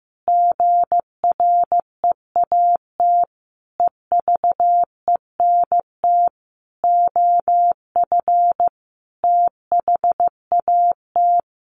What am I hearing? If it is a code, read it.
Morse code, heard as a keyed tone, about 15 wpm, 700 Hz. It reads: GREAT EVENT OF THAT
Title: Sentences from Top 1400 Words 15wpm